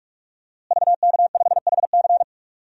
Morse code, heard as a keyed tone, about 46 wpm, 700 Hz. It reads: VK5HC
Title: DXCC-100 Call Signs - 5 Characters - Repeated 3x - 45wpm